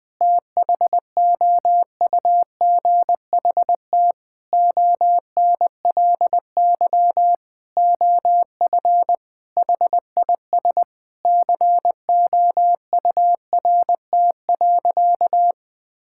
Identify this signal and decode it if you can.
Morse code, heard as a keyed tone, about 21 wpm, 700 Hz. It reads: THOUGHT ONLY OF HIS COURT.
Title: Sentences from Top 1600 Words 20wpm